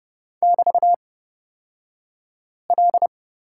Morse code, heard as a keyed tone, about 30 wpm, spaced as for 5 wpm, 690 Hz. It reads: <BT>L
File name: Lesson 36d - BT - Two Characters 30wpm